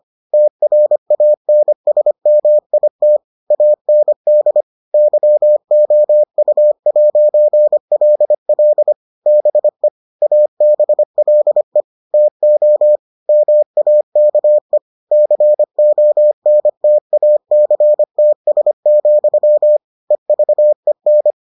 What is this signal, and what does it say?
Morse code, heard as a keyed tone, about 25 wpm, 600 Hz. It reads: TRANSMIT AND YOU'LL BE ABLE TO MAKE CONTACTS, EVEN